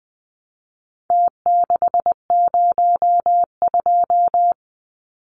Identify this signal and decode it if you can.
Morse code, heard as a keyed tone, about 20 wpm, 700 Hz. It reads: T602
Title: Numbers - Contest Style - 4-Digit 20wpm